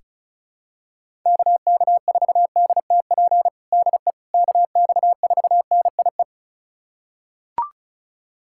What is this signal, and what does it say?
Morse code, heard as a keyed tone, about 36 wpm, 700 Hz. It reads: KK4DTP DE KX4NIE
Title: USA Callsign DE Callsign - 6 Characters - 35wpm